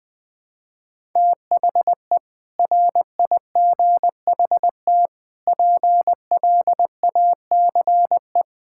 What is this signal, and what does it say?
Morse code, heard as a keyed tone, about 20 wpm, 700 Hz. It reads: THE RIGHT PLACE
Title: Sets of 6 Words - 20wpm